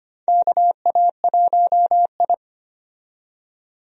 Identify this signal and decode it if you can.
Morse code, heard as a keyed tone, about 25 wpm, 700 Hz. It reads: KA1I
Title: USA Call Signs - 4 Characters - 25wpm